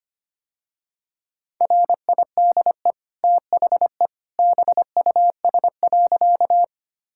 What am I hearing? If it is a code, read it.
Morse code, heard as a keyed tone, about 25 wpm, 700 Hz. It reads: RIDE THE BUS.
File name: Easy Sentences with Hints - 25wpm